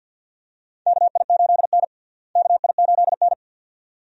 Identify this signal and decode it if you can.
Morse code, heard as a keyed tone, about 50 wpm, 700 Hz. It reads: KI8N KI8N
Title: USA Call Signs - 4 Characters - Repeated 2x - 50wpm